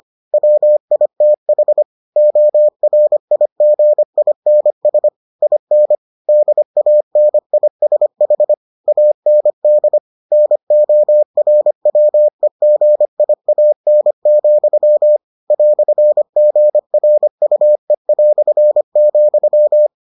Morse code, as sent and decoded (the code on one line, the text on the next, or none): .-- .. - .... / --- .-. .. --. .. -. ... / .. -. / -.. .- -. .. ... .... / .- -. -.. / -. --- .-. .-- . --. .. .- -. --..-- / .-..-. --. .-. ..- . .-..-. --..--
WITH ORIGINS IN DANISH AND NORWEGIAN, "GRUE",